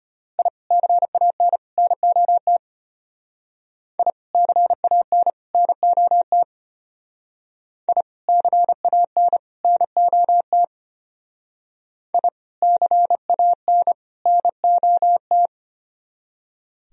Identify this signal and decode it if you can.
Morse code, heard as a keyed tone, about 32 wpm, 700 Hz. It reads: I CAN NOT I CAN NOT I CAN NOT I CAN NOT